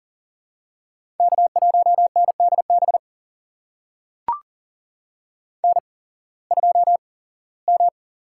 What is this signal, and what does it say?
Morse code, heard as a keyed tone, about 40 wpm, spaced as for 21 wpm, 700 Hz. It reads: K1NDB N2M
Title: USA Call Signs - 5 Characters - 8x Character Spacing - 40wpm